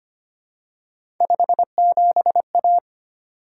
Morse code, heard as a keyed tone, about 25 wpm, 700 Hz. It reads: 57A
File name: Three Letter-Number Combinations - 25wpm